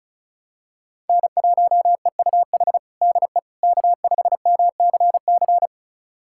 Morse code, heard as a keyed tone, about 34 wpm, 700 Hz. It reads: N1EUH DE K5MCC
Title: USA Callsign DE Callsign - 5 Characters - 35wpm